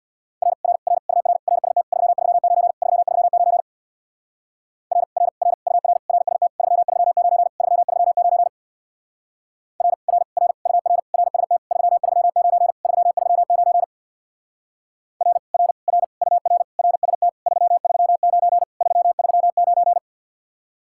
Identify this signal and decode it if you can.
Morse code, heard as a keyed tone, about 64 wpm, 700 Hz. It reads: R R R UR RST 339 339 R R R UR RST 339 339 R R R UR RST 339 339 R R R UR RST 339 339